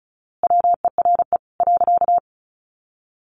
Morse code, heard as a keyed tone, about 34 wpm, 700 Hz. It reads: WERE .
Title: A Sound Of Thunder 35wpm